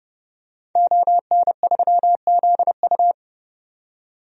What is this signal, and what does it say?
Morse code, heard as a keyed tone, about 30 wpm, 700 Hz. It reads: ON3ZU
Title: DXCC-100 Call Signs - 5 Characters - 30wpm